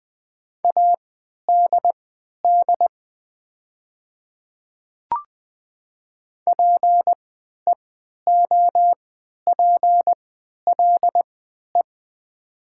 Morse code, heard as a keyed tone, about 20 wpm, spaced as for 11 wpm, 690 Hz. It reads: ADD PEOPLE
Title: 3x Character Spacing - Top 200 Words - 20wpm